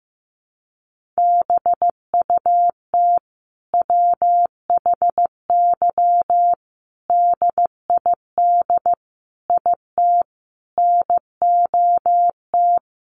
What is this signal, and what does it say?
Morse code, heard as a keyed tone, about 15 wpm, 700 Hz. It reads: BUT WHY DID IT NOT
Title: Sentences from Top 1400 Words 15wpm